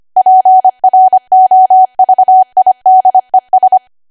Morse code, heard as a keyed tone, about 25 wpm, 750 Hz. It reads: PROVIDES
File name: Archive / 25 WPM / 25WPM